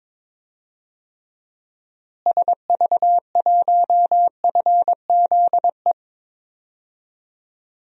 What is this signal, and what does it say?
Morse code, heard as a keyed tone, about 22 wpm, 700 Hz. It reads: SV1FZE
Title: DXCC-100 Call Signs - 6 Characters - Repeated 3x - 22wpm